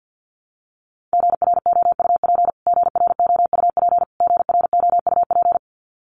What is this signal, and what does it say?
Morse code, heard as a keyed tone, about 50 wpm, 700 Hz. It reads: GROUP GROUP GROUP
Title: Top 300 Words Repeated 50wpm